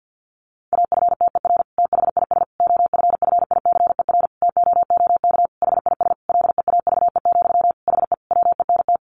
Morse code, heard as a keyed tone, about 48 wpm, 700 Hz. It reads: AFTER THIS OFFICER TOOK HIS LEAVE, HE WENT
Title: Sentences from Top 1200 Words 50wpm